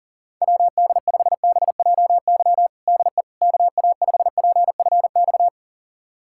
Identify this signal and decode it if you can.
Morse code, heard as a keyed tone, about 40 wpm, 700 Hz. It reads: WD5BJY DE KA5PFX
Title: USA Callsign DE Callsign - 6 Characters - 40wpm